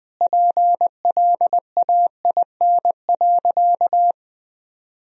Morse code, heard as a keyed tone, about 20 wpm, 700 Hz. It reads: PLAIN.
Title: Sentences from Top 400 Words and Abbreviations 20wpm